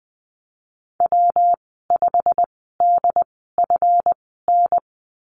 Morse code, heard as a keyed tone, about 20 wpm, spaced as for 15 wpm, 700 Hz. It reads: W5DFN
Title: 140 Lesson 23g - Letter f - Call Signs 20wpm